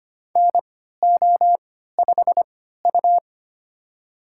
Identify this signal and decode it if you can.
Morse code, heard as a keyed tone, about 25 wpm, spaced as for 14 wpm, 700 Hz. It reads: NO5U
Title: USA Call Signs - 4 Characters - 3x Character Spacing - 25wpm